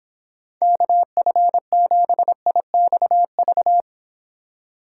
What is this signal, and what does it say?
Morse code, heard as a keyed tone, about 26 wpm, 700 Hz. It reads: KF7IXV